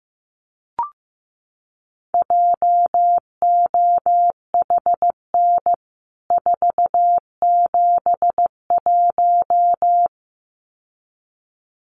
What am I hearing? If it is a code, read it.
Morse code, heard as a keyed tone, about 15 wpm, 700 Hz. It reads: JOHN 471
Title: CWT Exchange 15wpm